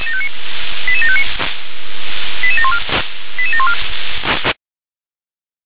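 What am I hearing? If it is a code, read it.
Five-tone selective call (ZVEI), heard as quick runs of five tones, tones of about 70 ms. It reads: 88759 88714 88714
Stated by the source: ZVEI five-tone selective call received on SDR.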